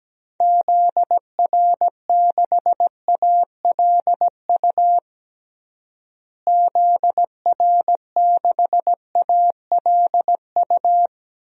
Morse code, heard as a keyed tone, about 17 wpm, 700 Hz. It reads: ZR6ALU ZR6ALU
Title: DXCC-100 Call Signs - 6 Characters - Repeated 3x - 17wpm